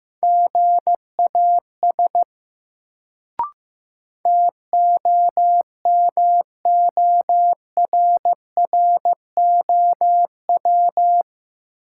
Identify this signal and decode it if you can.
Morse code, heard as a keyed tone, about 15 wpm, 700 Hz. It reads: GAS TOMORROW
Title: Sentences from Top 1100 Words - New Word Drill - 15wpm